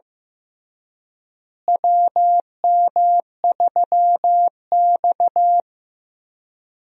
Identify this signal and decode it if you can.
Morse code, heard as a keyed tone, about 16 wpm, 700 Hz. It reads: WM3X
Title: USA Call Signs - 4 Characters - Repeated 3x - 15wpm